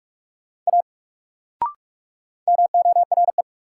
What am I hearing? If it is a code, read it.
Morse code, heard as a keyed tone, about 42 wpm, 700 Hz. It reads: A MORE
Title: Sentences from Top 100 Words - New Word Drill - 45wpm